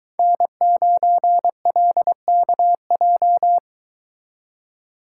Morse code, heard as a keyed tone, about 22 wpm, 700 Hz. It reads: N9LKJ